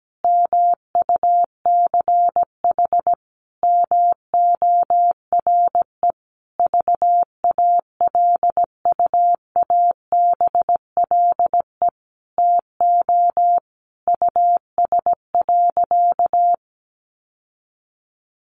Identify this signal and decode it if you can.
Morse code, heard as a keyed tone, about 17 wpm, 700 Hz. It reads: MUCH MORE VALUABLE TO US.
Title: Sentences from Top 1100 Words 17wpm